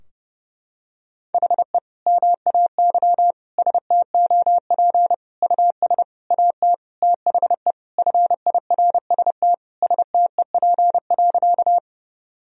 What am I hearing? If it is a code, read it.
Morse code, heard as a keyed tone, about 30 wpm, 700 Hz. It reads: HE MAY STOP US AT THE FIRST STEP.
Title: Sentences from Top 1000 Words - Review Part 2 - 30wpm